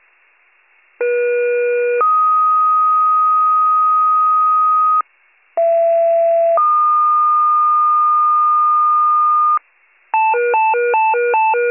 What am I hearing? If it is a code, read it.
Two-tone page, heard as a one-second tone, then a three-second tone, about 480 then 1250 Hz, about 670 then 1220 Hz. It is heard from 1.0 to 9.6 s.